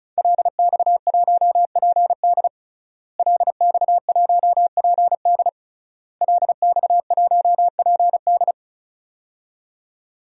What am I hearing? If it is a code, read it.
Morse code, heard as a keyed tone, about 36 wpm, 700 Hz. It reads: LX1PD LX1PD LX1PD
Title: DXCC-100 Call Signs - 5 Characters - Repeated 3x - 35wpm